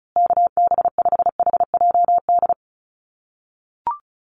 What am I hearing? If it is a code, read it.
Morse code, heard as a keyed tone, about 34 wpm, 700 Hz. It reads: KB5HJD
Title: USA Call Signs Repeated 2x 35wpm